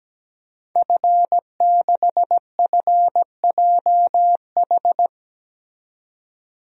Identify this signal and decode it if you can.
Morse code, heard as a keyed tone, about 17 wpm, 700 Hz. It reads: F6FJH
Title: DXCC-100 Call Signs - 5 Characters - 17wpm